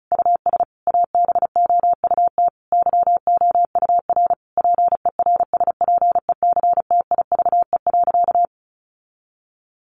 Morse code, heard as a keyed tone, about 34 wpm, 700 Hz. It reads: US ABOUT YOUR PERSPECTIVE.
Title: Sentences from Top 1200 Words 35wpm